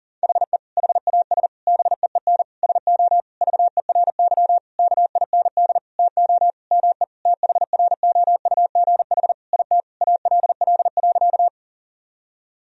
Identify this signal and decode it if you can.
Morse code, heard as a keyed tone, about 37 wpm, 700 Hz. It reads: HE HAS BEEN SO VERY KIND TO ME THROUGH IT ALL.
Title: Sentences from Top 1000 Words - Review Part 1 - 40wpm